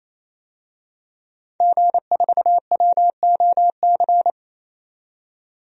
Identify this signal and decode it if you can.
Morse code, heard as a keyed tone, about 28 wpm, 700 Hz. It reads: G4WOC
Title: DXCC-100 Call Signs - 5 Characters - 28wpm